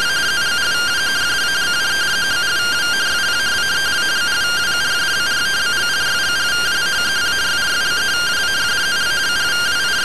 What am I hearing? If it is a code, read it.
Two-tone teleprinter signal, about 170 Hz shift.
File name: findings / CT SAGITAL 27.gen-16 rtty rx